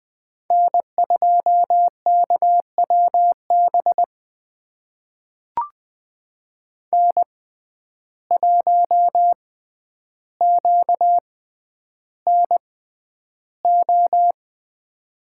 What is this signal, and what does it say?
Morse code, heard as a keyed tone, about 20 wpm, spaced as for 10 wpm, 700 Hz. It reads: N2KWB N1QNO
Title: USA Call Signs - 6x Character Spacing - 20wpm